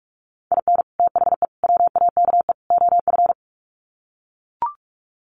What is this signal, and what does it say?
Morse code, heard as a keyed tone, about 44 wpm, 700 Hz. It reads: IN THE WAKE OF
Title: Sets of 4 Words 45wpm